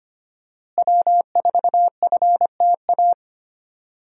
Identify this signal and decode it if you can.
Morse code, heard as a keyed tone, about 25 wpm, 700 Hz. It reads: W4FTA
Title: USA Call Signs - 5 Characters - Repeated 2x - 25wpm